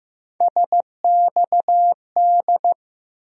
Morse code, heard as a keyed tone, about 15 wpm, 700 Hz. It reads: SXD
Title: USA Call Sign Postfixes - 3 Letters 15wpm